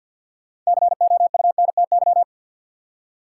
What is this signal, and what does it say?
Morse code, heard as a keyed tone, about 49 wpm, 700 Hz. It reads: COUNTY
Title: Six Letter Words - 50wpm